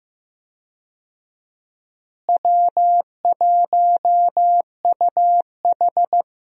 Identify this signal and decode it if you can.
Morse code, heard as a keyed tone, about 15 wpm, 700 Hz. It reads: W1UH
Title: USA Call Signs - 4 Characters - Repeated 2x - 15wpm